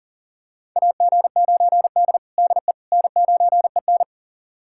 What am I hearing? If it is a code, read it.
Morse code, heard as a keyed tone, about 40 wpm, 700 Hz. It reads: AG9D DE N9EN
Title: USA Callsign DE Callsign - 4 Characters - 40wpm